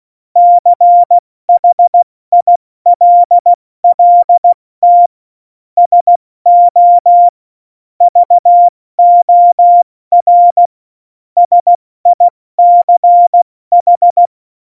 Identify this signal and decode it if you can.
Morse code, heard as a keyed tone, about 16 wpm, spaced as for 14 wpm, 700 Hz. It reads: CHILLT SO VOR SICH
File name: Morsekurs / Witze / Witz 08. Dezember 2020, 70 F80